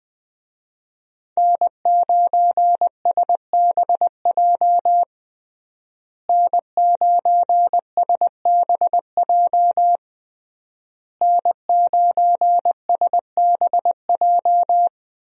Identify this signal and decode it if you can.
Morse code, heard as a keyed tone, about 20 wpm, 700 Hz. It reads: N9SBJ N9SBJ N9SBJ
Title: USA Call Signs - 5 Characters - Repeated 3x - 20wpm